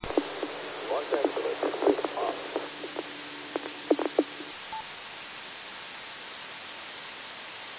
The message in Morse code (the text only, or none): none